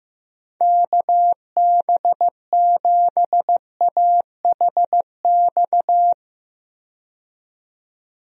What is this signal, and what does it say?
Morse code, heard as a keyed tone, about 15 wpm, 700 Hz. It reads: KB7AHX
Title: USA Call Signs - 6 Characters - 15wpm